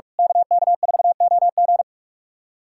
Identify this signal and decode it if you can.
Morse code, heard as a keyed tone, about 44 wpm, 700 Hz. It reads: KK4OG
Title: USA Callsign DE Callsign - 5 Characters - 45wpm